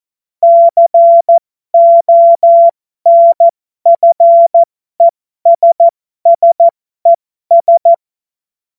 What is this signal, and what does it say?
Morse code, heard as a keyed tone, about 14 wpm, spaced as for 12 wpm, 680 Hz. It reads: CONFESSES